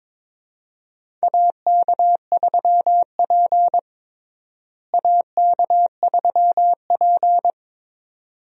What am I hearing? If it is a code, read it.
Morse code, heard as a keyed tone, about 22 wpm, 700 Hz. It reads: AK3P AK3P
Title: USA Call Signs - 4 Characters - Repeated 2x - 22wpm